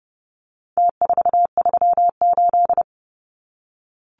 Morse code, heard as a keyed tone, about 30 wpm, 700 Hz. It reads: T438
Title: Numbers - Contest Style - 4-Digit 30wpm